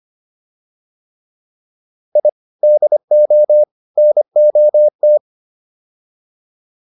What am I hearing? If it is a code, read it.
Morse code, heard as a keyed tone, about 25 wpm, 600 Hz. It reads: I DO NOT